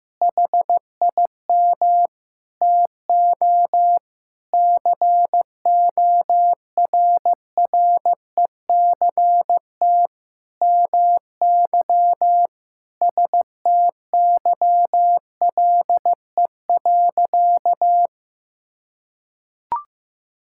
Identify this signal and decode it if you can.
Morse code, heard as a keyed tone, about 15 wpm, 700 Hz. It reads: HIM TO CORRECT MY STYLE.
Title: Sentences from Top 1000 Words - Review Part 5 - 15wpm